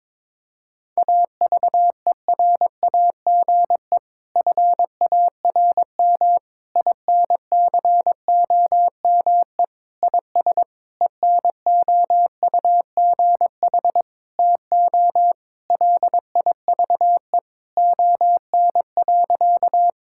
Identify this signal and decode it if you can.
Morse code, heard as a keyed tone, about 22 wpm, 700 Hz. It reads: AVERAGE FARM INCOME IS ENOUGH TO LIVE ON.
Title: Sentences from Top 1000 Words - Review Part 6 - 22wpm